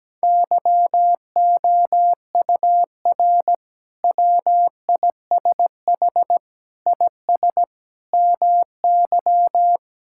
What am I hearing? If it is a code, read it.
Morse code, heard as a keyed tone, about 17 wpm, 700 Hz. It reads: YOUR WISH IS MY
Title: Sentences from Top 1600 Words 17wpm